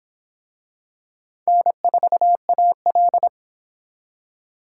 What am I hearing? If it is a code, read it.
Morse code, heard as a keyed tone, about 24 wpm, 700 Hz. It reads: N4AL